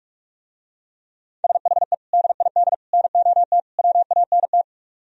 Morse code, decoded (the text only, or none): SHE DID NOT WANT